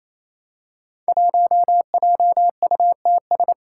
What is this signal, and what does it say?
Morse code, heard as a keyed tone, about 28 wpm, 700 Hz. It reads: 1JUTS
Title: Five Challenging Letter-Number Combinations 28wpm